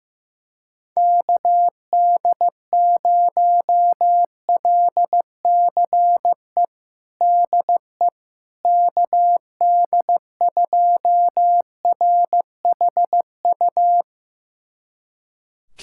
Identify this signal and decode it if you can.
Morse code, heard as a keyed tone, about 15 wpm, 700 Hz. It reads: KD0LCE DE KD2RHU
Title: USA Callsign DE Callsign - 6 Characters - 15wpm